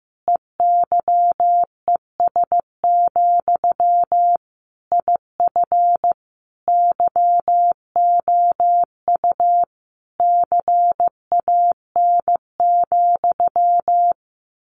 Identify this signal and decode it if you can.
Morse code, heard as a keyed tone, about 15 wpm, 700 Hz. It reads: EYES, IF YOU CAN,
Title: A Sound Of Thunder 15wpm